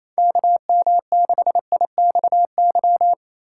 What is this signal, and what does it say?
Morse code, heard as a keyed tone, about 28 wpm, 700 Hz. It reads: KM6IXY
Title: USA Call Signs - 6 Characters - 8x Character Spacing - 28wpm